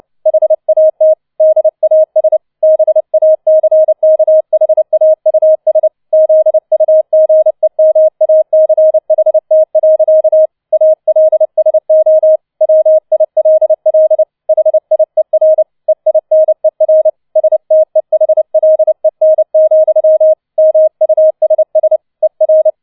Morse code, heard as a keyed tone, about 29 wpm, 630 Hz. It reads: HAT DAS BACKHAUS ZUGEMACHT. ALSO WILL HIER EINER STEHLEN, MUSS ER
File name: 006 Sechster Streich 30wpm